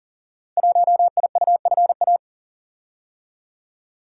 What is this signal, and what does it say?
Morse code, heard as a keyed tone, about 40 wpm, 700 Hz. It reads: 1IUFA
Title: Code Groups - 5 Letter:Numbers - 40wpm